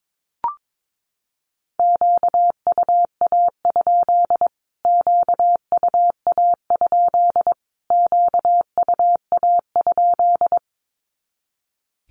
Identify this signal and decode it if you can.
Morse code, heard as a keyed tone, about 22 wpm, 700 Hz. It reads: QUA? QUA? QUA?
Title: Q-Codes Repeated 3x 22wpm